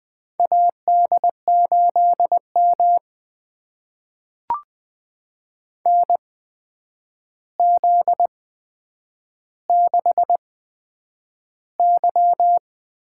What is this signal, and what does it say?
Morse code, heard as a keyed tone, about 20 wpm, spaced as for 9 wpm, 700 Hz. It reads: AD8M NZ6Y
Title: USA Call Signs - 4 Characters - 8x Character Spacing - 20wpm